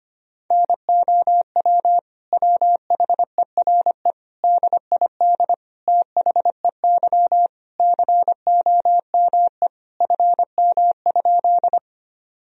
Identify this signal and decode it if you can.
Morse code, heard as a keyed tone, about 25 wpm, 700 Hz. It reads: NOW WHERE DID THEY COME FM?
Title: Sentences from Top 200 Words and Abbreviations 25wpm